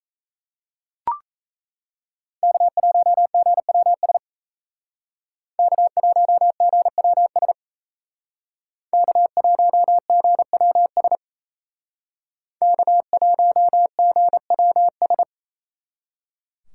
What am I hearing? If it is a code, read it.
Morse code, heard as a keyed tone, about 36 wpm, 700 Hz. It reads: K1GWS K1GWS K1GWS K1GWS